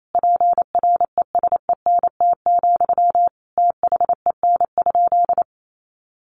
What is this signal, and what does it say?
Morse code, heard as a keyed tone, about 28 wpm, 700 Hz. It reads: PRESENT, THEN?
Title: Sentences from Top 800 Words 28wpm